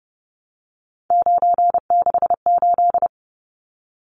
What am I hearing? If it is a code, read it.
Morse code, heard as a keyed tone, about 30 wpm, 700 Hz. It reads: 968
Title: Numbers - 3-Digit 30wpm